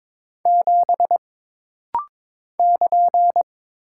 Morse code, heard as a keyed tone, about 22 wpm, 700 Hz. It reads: 7 <KN>
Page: This rapid-fire timing is geared to push your ICR (Instant Character Recognition) skills.